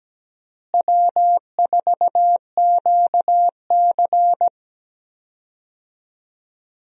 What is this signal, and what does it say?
Morse code, heard as a keyed tone, about 17 wpm, 700 Hz. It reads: W4QC
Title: USA Call Signs - 4 Characters - Repeated 2x - 17wpm